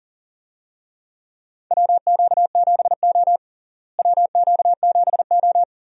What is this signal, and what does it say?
Morse code, heard as a keyed tone, about 40 wpm, 700 Hz. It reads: WQ7O WQ7O
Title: USA Call Signs - 4 Characters - Repeated 2x - 40wpm